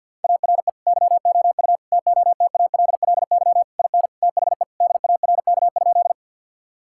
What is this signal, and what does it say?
Morse code, heard as a keyed tone, about 50 wpm, 700 Hz. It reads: ARE YOU TOTALLY IN THE DARK?
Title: Sentences from Top 1600 Words 50wpm